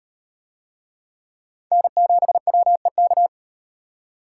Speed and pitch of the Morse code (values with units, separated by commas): 38 wpm, 700 Hz